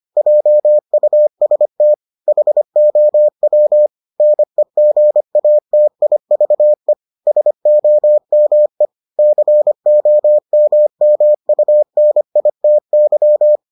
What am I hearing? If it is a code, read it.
Morse code, heard as a keyed tone, about 25 wpm, 600 Hz. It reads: JUST HOW NEGATIVE SOME COMMUNITY